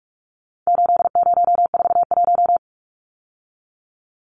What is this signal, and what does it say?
Morse code, heard as a keyed tone, about 44 wpm, 700 Hz. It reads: 8041
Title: Numbers - 4-Digit 45wpm